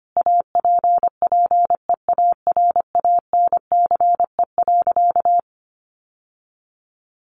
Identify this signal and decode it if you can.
Morse code, heard as a keyed tone, about 25 wpm, 700 Hz. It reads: APPEARANCE.
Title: Sentences from Top 1100 Words 25wpm